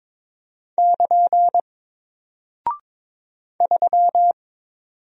Morse code, heard as a keyed tone, about 22 wpm, 700 Hz. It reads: <KN> 3
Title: Single Character (Letter:Number:Punctuation:Prosign) Alt 22wpm